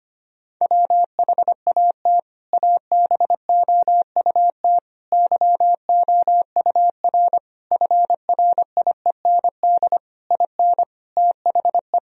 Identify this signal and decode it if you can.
Morse code, heard as a keyed tone, about 25 wpm, 700 Hz. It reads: WHAT ABOUT YOUR FRIEND IN THE
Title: Sentences from Top 1000 Words - Review Part 1 - 25wpm